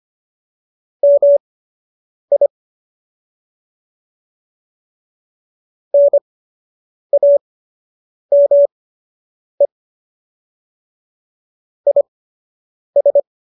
Morse code, heard as a keyed tone, about 25 wpm, spaced as for 8 wpm, 570 Hz. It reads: MI NAME IS
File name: qso006 08